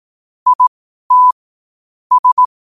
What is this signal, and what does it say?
Morse code, heard as a keyed tone, about 18 wpm, spaced as for 12 wpm, 1000 Hz. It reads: IT S
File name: qotdfiles / 12wpm